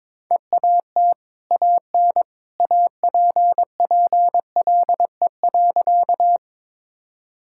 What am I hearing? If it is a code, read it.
Morse code, heard as a keyed tone, about 22 wpm, 690 Hz. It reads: EAT AN APPLE.
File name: practice / 22wpm / Easy Sentences with Hints - 22wpm